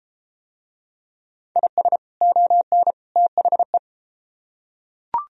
Morse code, heard as a keyed tone, about 34 wpm, 700 Hz. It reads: IS ON THE